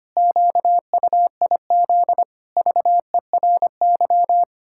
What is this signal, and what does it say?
Morse code, heard as a keyed tone, about 25 wpm, 700 Hz. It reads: QUIZ VERY
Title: Pangram Challenge v2 - 25wpm